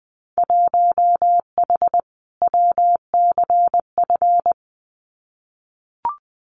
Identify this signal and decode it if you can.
Morse code, heard as a keyed tone, about 20 wpm, 700 Hz. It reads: 1H WCF
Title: Winter Field Day Exchanges 20wpm